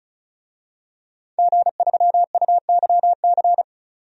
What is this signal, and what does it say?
Morse code, heard as a keyed tone, about 34 wpm, 700 Hz. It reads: G3UYC
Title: DXCC-100 Call Signs - 5 Characters - 35wpm